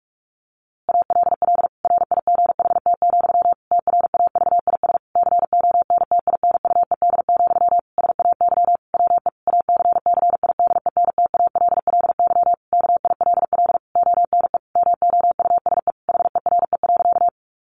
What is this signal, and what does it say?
Morse code, heard as a keyed tone, about 44 wpm, 700 Hz. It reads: ALL RIGHT, TRAVIS CONTINUED, SAY WE ACCIDENTALLY KILL ONE MOUSE HERE.
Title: A Sound Of Thunder 45wpm